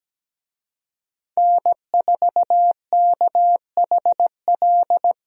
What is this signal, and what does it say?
Morse code, heard as a keyed tone, about 17 wpm, 700 Hz. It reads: N4KHL
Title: USA Call Signs - 5 Characters - Repeated 2x - 17wpm